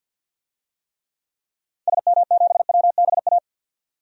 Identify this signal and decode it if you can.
Morse code, heard as a keyed tone, about 46 wpm, 700 Hz. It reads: SM7WBA
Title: DXCC-100 Call Signs - 6 Characters - 50wpm